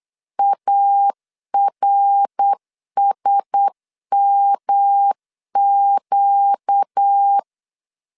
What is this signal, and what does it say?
Morse code, heard as a keyed tone, about 8 wpm, 800 Hz. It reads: ARSMQ
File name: LEZIONI / 6 PKX0 / 6 PK 8wpm 3003